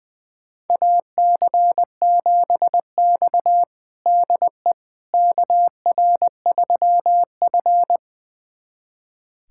Morse code, heard as a keyed tone, about 20 wpm, 700 Hz. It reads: AC7X DE KR3F
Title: USA Callsign DE Callsign - 4 Characters - 20wpm